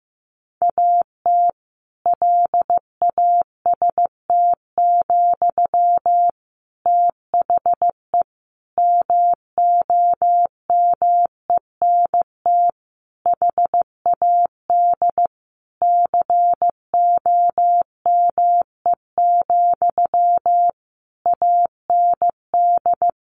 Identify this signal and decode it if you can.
Morse code, heard as a keyed tone, about 15 wpm, 700 Hz. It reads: AT LAST, THE MOMENT HAD COME, AND
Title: Sentences from Top 1500 Words 15wpm